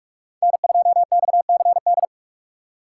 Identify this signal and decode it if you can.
Morse code, heard as a keyed tone, about 44 wpm, 700 Hz. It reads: N2XCD
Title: USA Call Signs - 2x Character Spacing - 45wpm